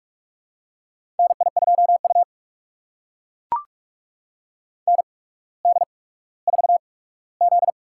Morse code, heard as a keyed tone, about 43 wpm, spaced as for 19 wpm, 700 Hz. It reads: NI2U ND4Z
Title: USA Call Signs - 4 Characters - 8x Character Spacing - 45wpm